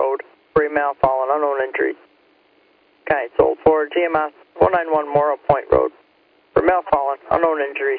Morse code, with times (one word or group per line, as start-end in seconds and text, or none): none